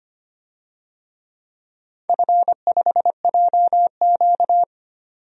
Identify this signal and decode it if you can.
Morse code, heard as a keyed tone, about 25 wpm, 690 Hz. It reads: F5JQ
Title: DXCC-100 Call Signs - 4 Characters - 25wpm